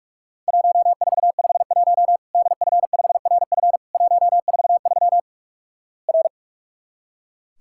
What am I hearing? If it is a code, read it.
Morse code, heard as a keyed tone, about 46 wpm, 700 Hz. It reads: 1451 DF5RF 143 R
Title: Worked All Europe DX Contest - QTC Exchange - 45wpm